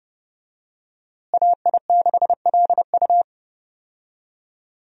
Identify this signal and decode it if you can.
Morse code, heard as a keyed tone, about 29 wpm, 700 Hz. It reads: AI6LU
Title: USA Call Signs - 5 Characters - Repeated 3x - 30wpm